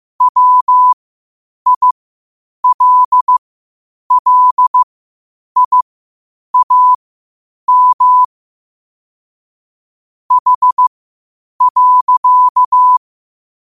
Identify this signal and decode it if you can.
Morse code, heard as a keyed tone, about 16 wpm, spaced as for 8 wpm, 1000 Hz. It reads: WILLIAM H.
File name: qotdfiles / 8wpm